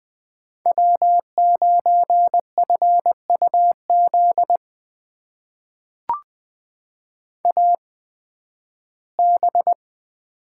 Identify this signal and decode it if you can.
Morse code, heard as a keyed tone, about 20 wpm, spaced as for 13 wpm, 700 Hz. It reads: W9FUZ AB
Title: USA Call Signs - 5 Characters - 8x Character Spacing - 20wpm